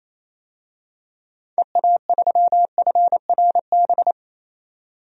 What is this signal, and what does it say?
Morse code, heard as a keyed tone, about 28 wpm, 700 Hz. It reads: EA3FRB
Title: DXCC-100 Call Signs - 6 Characters - 28wpm